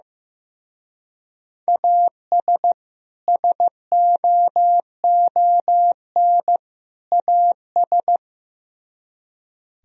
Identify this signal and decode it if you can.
Morse code, heard as a keyed tone, about 17 wpm, 700 Hz. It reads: AS SOON AS
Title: Speed-Racing - Sets of 3 Words 15wpm